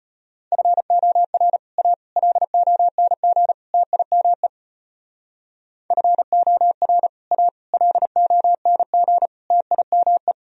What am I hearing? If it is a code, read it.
Morse code, heard as a keyed tone, about 31 wpm, 700 Hz. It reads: FOR A LONG TIME FOR A LONG TIME
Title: Speed-Racing - Sets of 4 Words 25wpm